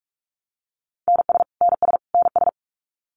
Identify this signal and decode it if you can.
Morse code, heard as a keyed tone, about 44 wpm, 700 Hz. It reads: NS NS NS
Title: Canadian Provinces and Territories - Repeated 3x - 45wpm